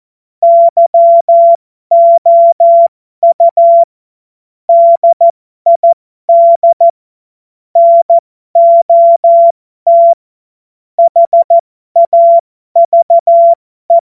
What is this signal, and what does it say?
Morse code, heard as a keyed tone, about 14 wpm, spaced as for 12 wpm, 680 Hz. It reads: YOU DID NOT HAVE